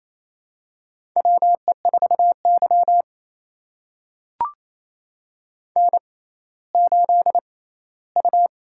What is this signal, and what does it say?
Morse code, heard as a keyed tone, about 28 wpm, spaced as for 16 wpm, 700 Hz. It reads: WE4Y N8U
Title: USA Call Signs - 4 Characters - 6x Character Spacing - 28wpm